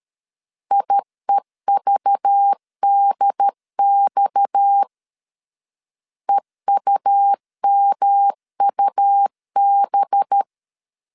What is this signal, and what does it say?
Morse code, heard as a keyed tone, about 13 wpm, 800 Hz. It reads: IEVDX EUMUB
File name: CW / DET / 1 6 1500C S B 12wpm 3431